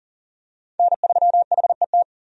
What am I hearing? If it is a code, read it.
Morse code, heard as a keyed tone, about 40 wpm, 700 Hz. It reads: N3HET
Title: USA Call Signs - 5 Characters - 40wpm